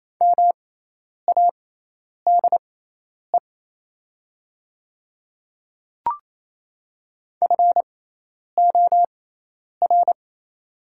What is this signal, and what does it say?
Morse code, heard as a keyed tone, about 28 wpm, spaced as for 10 wpm, 700 Hz. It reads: MADE FOR
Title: 6x Character Spacing - Top 200 Words - 28wpm